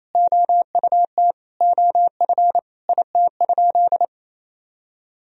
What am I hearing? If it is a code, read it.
Morse code, heard as a keyed tone, about 28 wpm, 700 Hz. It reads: OUT OF IT?
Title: Sentences from Top 1000 Words - Review Part 4 - 28wpm